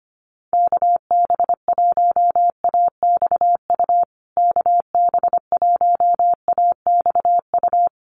Morse code, heard as a keyed tone, about 25 wpm, 700 Hz. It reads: KB1AXU KB1AXU
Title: USA Call Signs Repeated 3x 25wpm